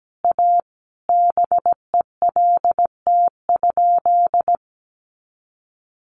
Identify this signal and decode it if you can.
Morse code, heard as a keyed tone, about 17 wpm, 700 Hz. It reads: A BELT?
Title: Sentences from Top 1400 Words 17wpm